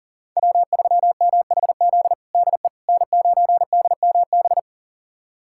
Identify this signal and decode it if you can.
Morse code, heard as a keyed tone, about 40 wpm, 700 Hz. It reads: W3MHZ DE N9DMB
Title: USA Callsign DE Callsign - 5 Characters - 40wpm